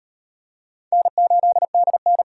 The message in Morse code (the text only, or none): N8DN